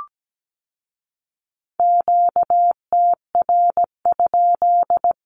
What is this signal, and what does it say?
Morse code, heard as a keyed tone, about 17 wpm, 700 Hz. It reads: QTR?
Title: Q-Codes with Spoken Meaning - 17wpm